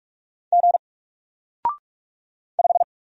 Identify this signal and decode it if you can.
Morse code, heard as a keyed tone, about 44 wpm, 700 Hz. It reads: G 5
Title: Single Character (Letter:Number:Punctuation:Prosign) Alt 45wpm